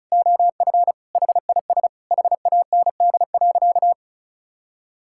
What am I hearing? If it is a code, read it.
Morse code, heard as a keyed tone, about 34 wpm, 700 Hz. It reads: OF HIS HAND.
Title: Sentences from Top 1600 Words 35wpm